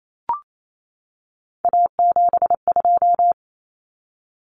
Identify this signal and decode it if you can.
Morse code, heard as a keyed tone, about 28 wpm, 700 Hz. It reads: A72
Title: Common Cut Numbers - 3-Digit 28wpm